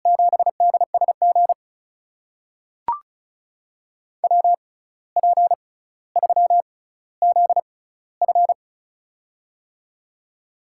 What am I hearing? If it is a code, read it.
Morse code, heard as a keyed tone, about 36 wpm, spaced as for 17 wpm, 700 Hz. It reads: 7DSG WP3ZF
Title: USA Call Signs - 5 Characters - 6x Character Spacing - 35wpm